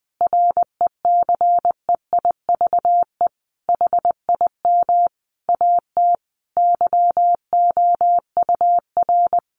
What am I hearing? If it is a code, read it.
Morse code, heard as a keyed tone, about 20 wpm, 700 Hz. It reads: RECEIVE HIM AT YOUR
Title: Sentences from Top 1000 Words 20wpm